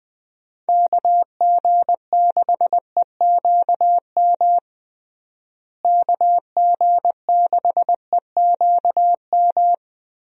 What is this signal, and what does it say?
Morse code, heard as a keyed tone, about 20 wpm, 700 Hz. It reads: KG6EQM KG6EQM
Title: USA Call Signs - 6 Characters - Repeated 3x - 20wpm